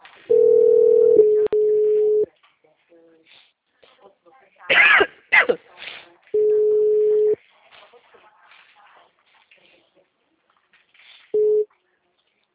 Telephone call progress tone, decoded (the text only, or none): ringback tone